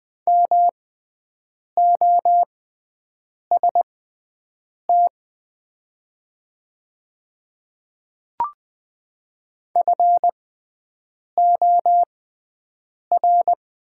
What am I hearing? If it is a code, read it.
Morse code, heard as a keyed tone, about 20 wpm, spaced as for 7 wpm, 700 Hz. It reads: MOST FOR
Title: 6x Character Spacing - Top 300 Words - 20wpm